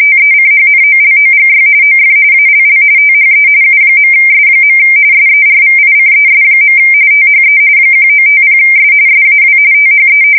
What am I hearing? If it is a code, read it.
RTTY, about 45.45 baud. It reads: TECHNOLOGY OF ALL THE U.S. SERVICES. THE 1950S SAW THE
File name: fsk170 / usnavy5